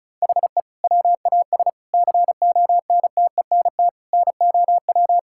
Morse code, heard as a keyed tone, about 34 wpm, 700 Hz. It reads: HE WAS CONTENT NOW
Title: Sentences from Top 1000 Words - Review Part 6 - 35wpm